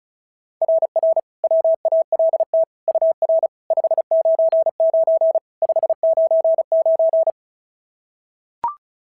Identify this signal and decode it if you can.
Morse code, heard as a keyed tone, about 33 wpm, 690 Hz. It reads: RR WALT UR 599 599
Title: SOTA QSO Practice 35wpm